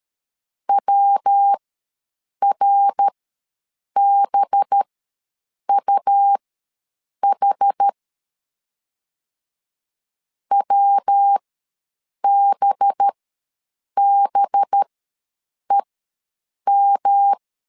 Morse code, heard as a keyed tone, about 13 wpm, spaced as for 7 wpm, 800 Hz. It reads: WRBUH WBBEM
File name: 5 LB ws 12wpm